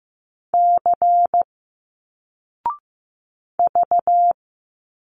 Morse code, heard as a keyed tone, about 15 wpm, 700 Hz. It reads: C V
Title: Single Letters 15wpm